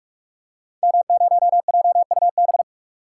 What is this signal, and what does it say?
Morse code, heard as a keyed tone, about 46 wpm, 700 Hz. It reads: M0JUB
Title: Code Groups - 5 Letter:Numbers - 45wpm